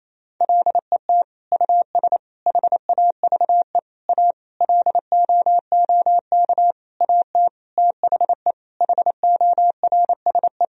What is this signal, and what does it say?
Morse code, heard as a keyed tone, about 28 wpm, 700 Hz. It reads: LET US HAVE A LOOK AT THE HORSE
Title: Sentences from Top 1000 Words - Review Part 1 - 28wpm